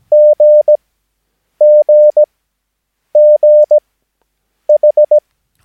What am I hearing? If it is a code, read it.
Morse code, heard as a keyed tone, about 17 wpm, spaced as for 8 wpm, 600 Hz. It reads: GGGH
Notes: Övningsmaterial med alfabetet A till Ö samt siffror 0 till 9 enligt International/Continental Morse Code.